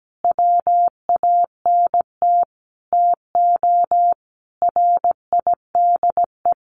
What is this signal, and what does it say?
Morse code, heard as a keyed tone, about 17 wpm, 700 Hz. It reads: WANT TO RIDE
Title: Sentences from Top 1200 Words 17wpm